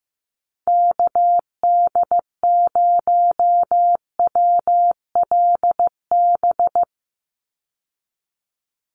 Morse code, heard as a keyed tone, about 15 wpm, 700 Hz. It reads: KD0WLB
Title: USA Call Signs 15wpm